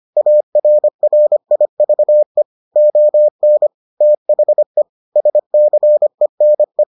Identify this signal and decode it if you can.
Morse code, heard as a keyed tone, about 25 wpm, 600 Hz. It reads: ARRIVE ON THE SCENE